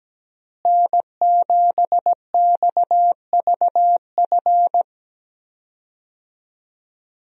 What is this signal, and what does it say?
Morse code, heard as a keyed tone, about 17 wpm, 700 Hz. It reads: N7XVF
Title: USA Call Signs - 5 Characters - Repeated 2x - 17wpm